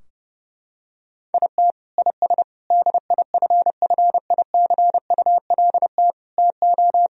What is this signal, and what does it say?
Morse code, heard as a keyed tone, about 30 wpm, 700 Hz. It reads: IT IS DIFFICULT TO
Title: Sentences from Top 1000 Words - Review Part 3 - 30wpm